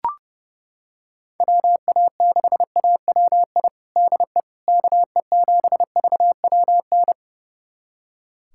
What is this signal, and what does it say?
Morse code, heard as a keyed tone, about 30 wpm, 700 Hz. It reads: WA6AWI DE KE7VWN
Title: USA Callsign DE Callsign - 6 Characters - 30wpm